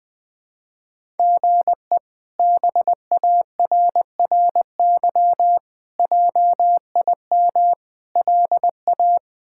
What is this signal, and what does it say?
Morse code, heard as a keyed tone, about 21 wpm, 700 Hz. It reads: GE BARRY JIM LA
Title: K1USN SST QSO Practice - 20wpm